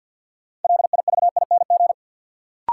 Morse code, heard as a keyed tone, about 51 wpm, 700 Hz. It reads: LIVING
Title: Sentences from Top 1100 Words - New Word Drill Alt - 50wpm